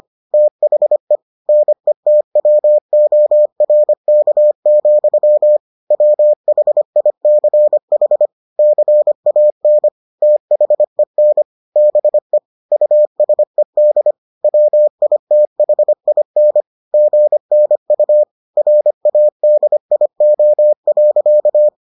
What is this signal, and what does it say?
Morse code, heard as a keyed tone, about 25 wpm, 600 Hz. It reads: THE NETWORK, WHICH CAN THEN BE USED WITHIN GNU RADIO.